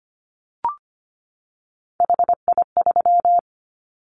Morse code, heard as a keyed tone, about 25 wpm, 700 Hz. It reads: HI3
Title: Callsign Prefixes for Top 100 DXCC - 25wpm